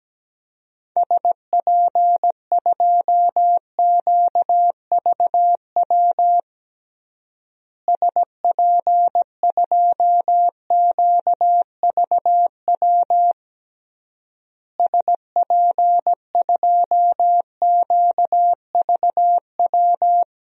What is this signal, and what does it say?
Morse code, heard as a keyed tone, about 17 wpm, 700 Hz. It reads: SP2QVW SP2QVW SP2QVW
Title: DXCC-100 Call Signs - 6 Characters - Repeated 3x - 17wpm